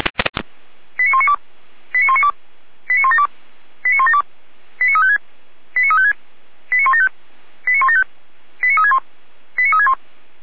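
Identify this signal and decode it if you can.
Five-tone selective call (ZVEI), heard as quick runs of five tones, tones of about 70 ms. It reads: 89192 89192 89172 89172 89356 89356 89175 89175 89361 89361
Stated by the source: Received on an SDR, ZVEI five-tone call 89500, from the feuerwehr-orlamuende.org audio archive (Feuerwehr Orlamünde).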